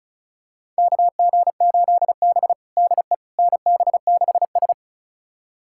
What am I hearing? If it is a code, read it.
Morse code, heard as a keyed tone, about 36 wpm, 690 Hz. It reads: KG8B DE NB6S
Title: USA Callsign DE Callsign - 4 Characters - 35wpm